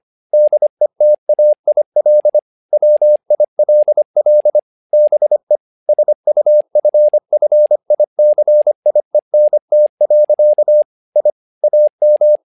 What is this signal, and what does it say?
Morse code, heard as a keyed tone, about 25 wpm, 600 Hz. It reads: DETAIL WILL BE SUFFICIENT. I AM
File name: learning center / tower trouble / tower trouble 25wpm